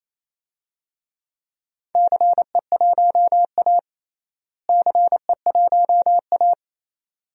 Morse code, heard as a keyed tone, about 27 wpm, 700 Hz. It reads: CE1A CE1A
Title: DXCC-100 Call Signs - 4 Characters - Repeated 2x - 28wpm